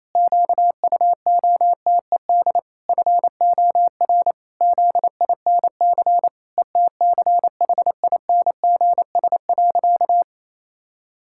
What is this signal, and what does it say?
Morse code, heard as a keyed tone, about 28 wpm, 700 Hz. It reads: QUOTED FOR ZINC ETCHINGS.
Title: Pangram Challenge v2 - 28wpm